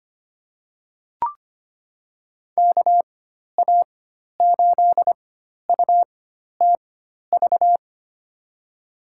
Morse code, heard as a keyed tone, about 25 wpm, spaced as for 12 wpm, 700 Hz. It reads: KA8UTV
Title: USA Call Signs - 6 Characters - 4x Character Spacing - 25wpm